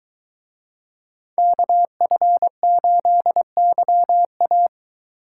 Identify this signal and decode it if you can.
Morse code, heard as a keyed tone, about 24 wpm, 700 Hz. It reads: KF8YA